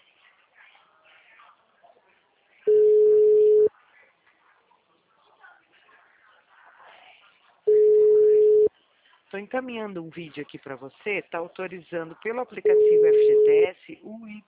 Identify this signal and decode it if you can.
Ringback tone (Europe, 425 Hz), with speech.